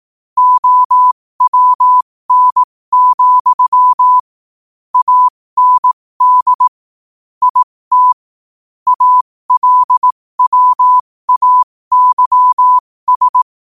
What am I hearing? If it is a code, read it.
Morse code, heard as a keyed tone, about 18 wpm, spaced as for 15 wpm, 1000 Hz. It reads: OWN, AND IT ALWAYS
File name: qotdfiles / 15wpm